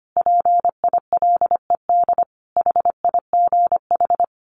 Morse code, heard as a keyed tone, about 25 wpm, 700 Hz. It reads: PILED HIGH
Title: A Sound Of Thunder 25wpm